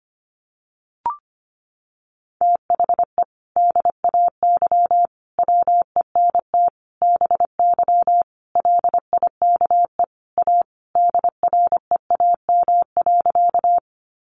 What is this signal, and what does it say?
Morse code, heard as a keyed tone, about 25 wpm, 700 Hz. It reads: THE DAY WENT BY LIKE A DREAM.
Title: Sentences from Top 1500 Words 25wpm